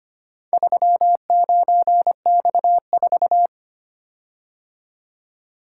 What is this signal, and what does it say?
Morse code, heard as a keyed tone, about 25 wpm, 700 Hz. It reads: 39X4
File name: practice / 25wpm / Four Letter-Number Combinations - 25wpm